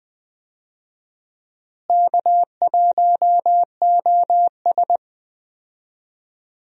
Morse code, heard as a keyed tone, about 20 wpm, 700 Hz. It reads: K1OS
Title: USA Call Signs - 4 Characters - Repeated 3x - 20wpm